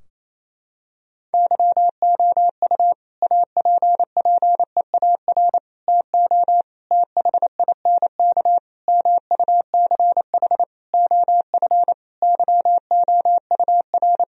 Morse code, heard as a keyed tone, about 28 wpm, 700 Hz. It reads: YOU APPEAR TO THINK MUCH OF YOUR
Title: Sentences from Top 1000 Words - Review Part 2 - 28wpm